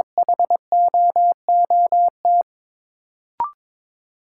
Morse code, heard as a keyed tone, about 21 wpm, 700 Hz. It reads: HOOT
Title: Sentences from Top 1200 Words - New Word Drill - 22wpm